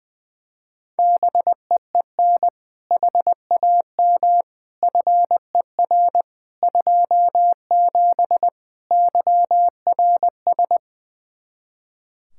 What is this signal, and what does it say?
Morse code, heard as a keyed tone, about 21 wpm, 700 Hz. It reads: BEEN HAM FER 27 YRS
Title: Practice QSO - Exchange 3 - More Info - 20wpm